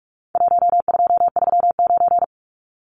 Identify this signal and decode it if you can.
Morse code, heard as a keyed tone, about 44 wpm, 700 Hz. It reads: 1239
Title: Numbers - 4-Digit 45wpm